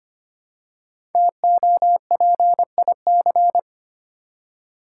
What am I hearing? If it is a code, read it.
Morse code, heard as a keyed tone, about 25 wpm, 700 Hz. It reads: TOPIC
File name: 5 Letter Words - First 3 Letters Given - 25wpm